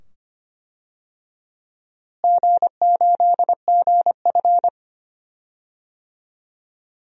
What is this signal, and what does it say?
Morse code, heard as a keyed tone, about 25 wpm, 700 Hz. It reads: G8GF
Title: DXCC-100 Call Signs - 4 Characters - Repeated 2x - 25wpm